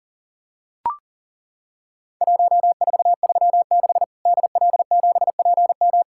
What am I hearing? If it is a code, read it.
Morse code, heard as a keyed tone, about 39 wpm, 700 Hz. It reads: 1436 DL7PM
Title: Worked All Europe DX Contest - QTC Exchange - 40wpm